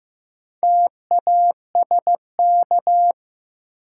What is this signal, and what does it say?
Morse code, heard as a keyed tone, about 15 wpm, 700 Hz. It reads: TASK
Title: Sentences from Top 1000 Words - Review Part 7 - New Word Drill - 15wpm